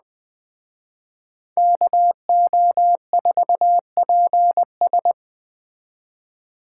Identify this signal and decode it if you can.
Morse code, heard as a keyed tone, about 20 wpm, 700 Hz. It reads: KO4PS
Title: Speed Racing - USA 2x2 Call Signs 17wpm